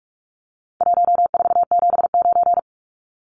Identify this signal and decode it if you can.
Morse code, heard as a keyed tone, about 44 wpm, 700 Hz. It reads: 1479
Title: Numbers - Contest Style - 4-Digit 45wpm